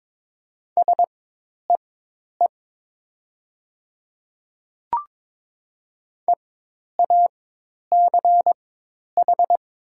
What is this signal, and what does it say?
Morse code, heard as a keyed tone, about 22 wpm, spaced as for 10 wpm, 700 Hz. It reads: SEE EACH
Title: 4x Character Spacing - Top 100 Words - 22wpm